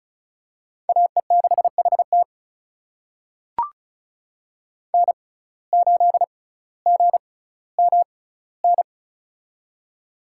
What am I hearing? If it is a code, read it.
Morse code, heard as a keyed tone, about 36 wpm, spaced as for 18 wpm, 700 Hz. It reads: AE6HT N8GMN
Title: USA Call Signs - 5 Characters - 6x Character Spacing - 35wpm